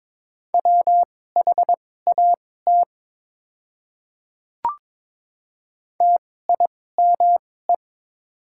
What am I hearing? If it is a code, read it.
Morse code, heard as a keyed tone, about 22 wpm, spaced as for 16 wpm, 690 Hz. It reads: WHAT TIME
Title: 2x Character Spacing - Top 100 Words - 22wpm